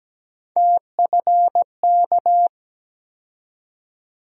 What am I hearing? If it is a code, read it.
Morse code, heard as a keyed tone, about 17 wpm, 700 Hz. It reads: TFK
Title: USA Call Sign Postfixes - 3 Letters 17wpm